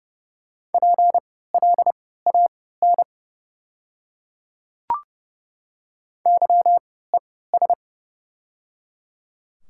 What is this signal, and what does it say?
Morse code, heard as a keyed tone, about 29 wpm, spaced as for 17 wpm, 700 Hz. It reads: PLAN YES
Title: 3x Character Spacing - Top 500 Words - 30wpm